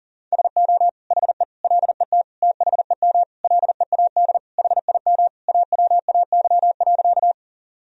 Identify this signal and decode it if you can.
Morse code, heard as a keyed tone, about 40 wpm, 700 Hz. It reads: SO HE LET THEM LEAD HIM AWAY.
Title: Sentences from Top 1000 Words - Review Part 2 - 40wpm